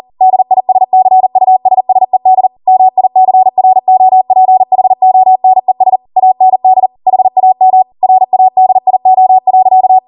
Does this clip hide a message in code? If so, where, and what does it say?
Yes: Morse code from 0.2 to 10.0 s, heard as a keyed tone, about 40 wpm, 750 Hz. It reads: DISCUSSED MICROPHONES AND HAM RADIO.